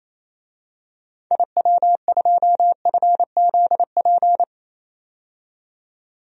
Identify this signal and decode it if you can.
Morse code, heard as a keyed tone, about 28 wpm, 700 Hz. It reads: IW2FZP
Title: DXCC-100 Call Signs - 6 Characters - 28wpm